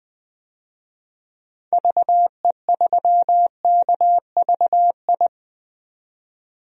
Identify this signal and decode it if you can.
Morse code, heard as a keyed tone, about 20 wpm, 700 Hz. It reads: VE3KVI
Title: DXCC-100 Call Signs - 6 Characters - 20wpm